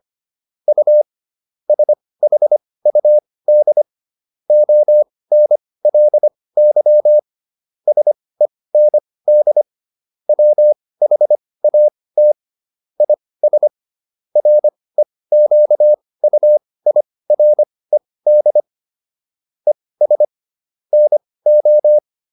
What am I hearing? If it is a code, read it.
Morse code, heard as a keyed tone, about 25 wpm, spaced as for 18 wpm, 600 Hz. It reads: U SHUD ONLY SEND WHAT IS REQUIRED ES NO